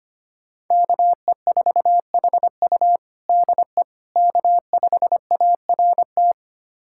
Morse code, heard as a keyed tone, about 25 wpm, 700 Hz. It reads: KE4HU DE K5ART
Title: USA Callsign DE Callsign - 5 Characters - 25wpm